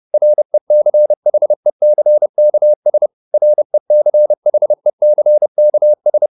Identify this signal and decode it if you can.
Morse code, heard as a keyed tone, about 30 wpm, 600 Hz. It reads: RECHECKS RECHECKS